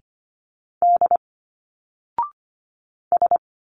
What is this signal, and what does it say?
Morse code, heard as a keyed tone, about 24 wpm, 700 Hz. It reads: D S
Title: Single Letters 25wpm